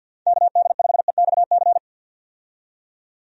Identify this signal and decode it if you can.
Morse code, heard as a keyed tone, about 50 wpm, 700 Hz. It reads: KD5EXC
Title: USA Call Signs - 6 Characters - 50wpm